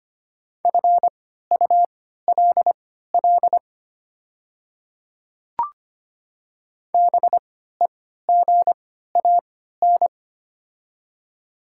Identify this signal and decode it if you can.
Morse code, heard as a keyed tone, about 25 wpm, spaced as for 14 wpm, 700 Hz. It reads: FULL BEGAN
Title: 3x Character Spacing - Top 500 Words - 25wpm